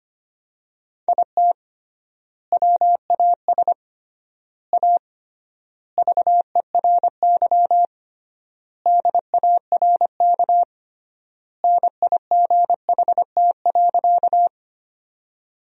Morse code, heard as a keyed tone, about 25 wpm, 700 Hz. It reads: IT WAS A VERY DARK NIGHT.
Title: 3x Word-Spacing - Sentences from Top 500 Words - 25wpm